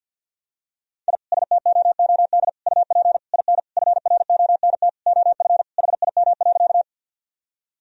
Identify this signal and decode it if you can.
Morse code, heard as a keyed tone, about 50 wpm, 700 Hz. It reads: I STOOD UP IN FRONT OF HIM.
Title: Sentences from Top 1000 Words - Review Part 2 - 50wpm